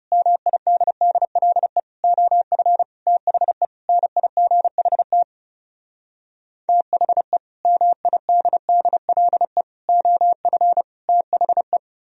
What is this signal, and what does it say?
Morse code, heard as a keyed tone, about 38 wpm, 700 Hz. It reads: MIDDLE OF THE NIGHT THE MIDDLE OF THE